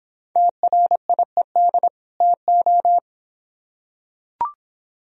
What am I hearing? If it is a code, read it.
Morse code, heard as a keyed tone, about 25 wpm, 700 Hz. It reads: TRIED TO